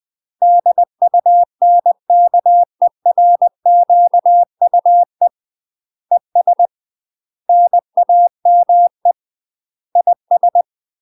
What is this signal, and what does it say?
Morse code, heard as a keyed tone, about 20 wpm, 700 Hz. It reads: DUNKERQUE ES NAME IS
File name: UK MASTER QSOs TRAINING / 20WPM / UK MASTER QSOs 0065